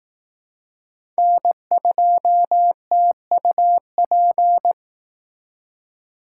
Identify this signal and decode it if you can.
Morse code, heard as a keyed tone, about 18 wpm, 700 Hz. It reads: N2TUP
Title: Speed Racing - USA 1x3 Call Signs 15wpm